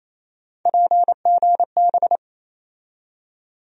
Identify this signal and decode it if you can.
Morse code, heard as a keyed tone, about 27 wpm, 700 Hz. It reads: PGB
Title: USA Call Sign Postfixes - 3 Letters 28wpm